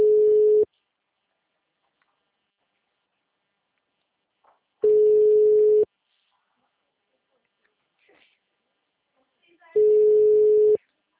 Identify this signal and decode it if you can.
Ringback tone (Europe, 425 Hz), with speech.
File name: out- -2018- - - .8723